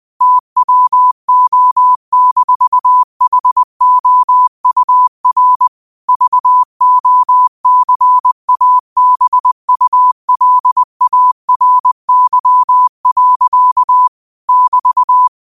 Morse code, heard as a keyed tone, about 20 wpm, 1000 Hz. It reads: TWO-HOUR VOCABULARY. <BT>
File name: qotdfiles / 20wpm